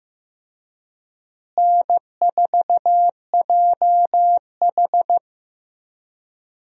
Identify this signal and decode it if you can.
Morse code, heard as a keyed tone, about 15 wpm, 700 Hz. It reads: N4JH
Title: USA Call Signs - 4 Characters - Repeated 3x - 15wpm